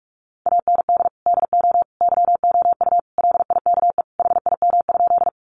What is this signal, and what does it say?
Morse code, heard as a keyed tone, about 44 wpm, 700 Hz. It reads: AND DO YOU LIKE HIM?
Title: Sentences from Top 100 Words 45wpm